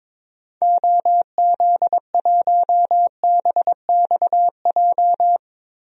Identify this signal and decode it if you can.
Morse code, heard as a keyed tone, about 22 wpm, 700 Hz. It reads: OZ1BXJ
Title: DXCC-100 Call Signs - 6 Characters - Repeated 3x - 22wpm